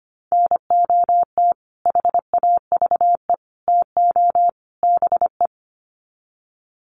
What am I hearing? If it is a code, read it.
Morse code, heard as a keyed tone, about 25 wpm, 700 Hz. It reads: NOT HAVE TO BE
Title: Sets of 4 Words 25wpm